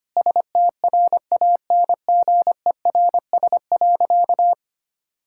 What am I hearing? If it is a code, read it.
Morse code, heard as a keyed tone, about 25 wpm, 700 Hz. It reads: STRANGERS.
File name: practice / 25wpm / Difficult Sentences with Hints - 25wpm